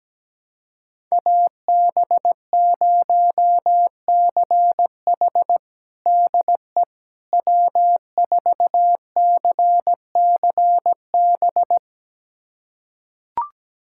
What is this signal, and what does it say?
Morse code, heard as a keyed tone, about 17 wpm, 700 Hz. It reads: AB0CH DE W4CCB
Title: USA Callsign DE Callsign - 5 Characters - 17wpm